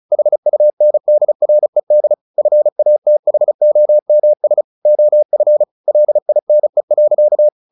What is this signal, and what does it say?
Morse code, heard as a keyed tone, about 34 wpm, 600 Hz. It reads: HUNDRED FATHOMS OF LINE.
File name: the old man and the sea / the old man and the sea 1 35wpm